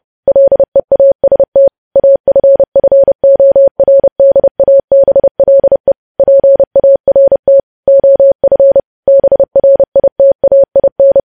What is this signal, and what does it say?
Morse code, heard as a keyed tone, about 30 wpm, 550 Hz. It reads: LEAST AFFORDABLE PART OF BRITAIN
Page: news headlines in morse code at 30 wpm